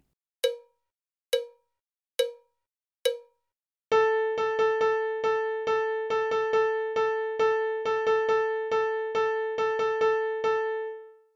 There is no code.